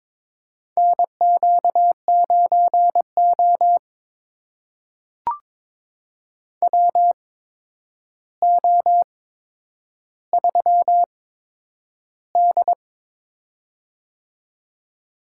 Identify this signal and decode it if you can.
Morse code, heard as a keyed tone, about 22 wpm, spaced as for 9 wpm, 700 Hz. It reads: NQ9O WO3D
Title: USA Call Signs - 4 Characters - 8x Character Spacing - 22wpm